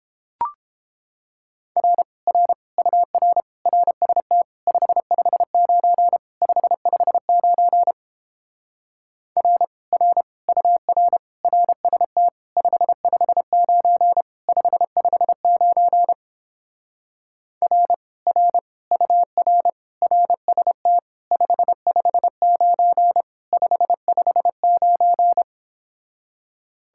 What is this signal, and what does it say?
Morse code, heard as a keyed tone, about 29 wpm, 700 Hz. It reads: R R UR RST 559 559 R R UR RST 559 559 R R UR RST 559 559